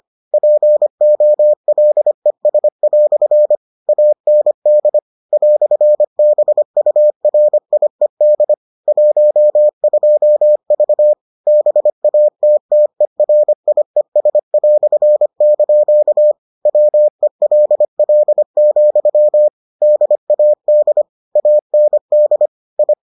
Morse code, heard as a keyed tone, about 25 wpm, 600 Hz. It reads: POLES" AND "BURIED 12V BATTERIES") WELL, DAD AND I